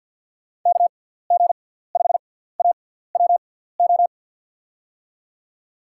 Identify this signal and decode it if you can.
Morse code, heard as a keyed tone, about 53 wpm, spaced as for 17 wpm, 700 Hz. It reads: KG5AWO
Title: USA Call Signs - 6 Characters - 6x Character Spacing - 50wpm